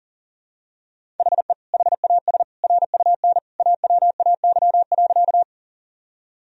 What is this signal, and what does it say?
Morse code, heard as a keyed tone, about 40 wpm, 700 Hz. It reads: HE HAS RUN AWAY.
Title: Sentences from Top 1000 Words - Review Part 1 - 40wpm